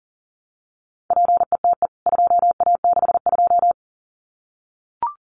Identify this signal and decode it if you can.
Morse code, heard as a keyed tone, about 40 wpm, 700 Hz. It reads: PETE 2A62
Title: CWT Exchange 40wpm